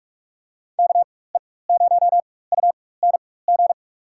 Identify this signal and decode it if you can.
Morse code, heard as a keyed tone, about 47 wpm, spaced as for 21 wpm, 700 Hz. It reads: KE0UNG
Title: USA Call Signs - 6 Characters - 4x Character Spacing - 45wpm